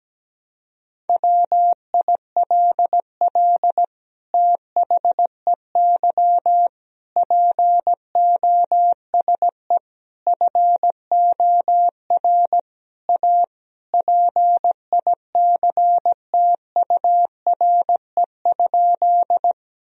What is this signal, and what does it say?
Morse code, heard as a keyed tone, about 17 wpm, 700 Hz. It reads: WILL THEY POSE FOR A PICTURE?
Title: Sentences from Top 1000 Words - Review Part 2 - 17wpm